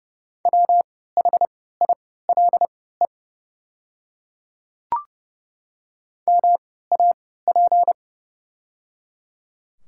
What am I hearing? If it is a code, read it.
Morse code, heard as a keyed tone, about 30 wpm, spaced as for 17 wpm, 700 Hz. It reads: WHILE MAP
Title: 3x Character Spacing - Top 500 Words - 30wpm